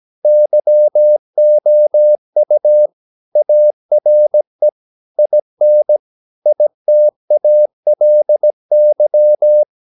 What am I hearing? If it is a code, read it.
Morse code, heard as a keyed tone, about 17 wpm, 600 Hz. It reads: YOU ARE IN ITALY
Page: Romanen Never af Ken Follet på engelsk.